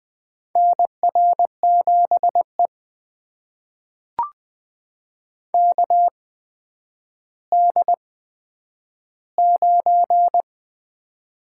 Morse code, heard as a keyed tone, about 20 wpm, spaced as for 10 wpm, 700 Hz. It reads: NR7E KD9
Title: USA Call Signs - 4 Characters - 8x Character Spacing - 20wpm